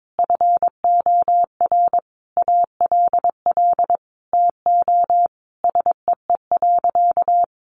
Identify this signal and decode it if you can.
Morse code, heard as a keyed tone, about 22 wpm, 700 Hz. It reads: FOR ALL TO SEE.
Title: Sentences from Top 1200 Words 22wpm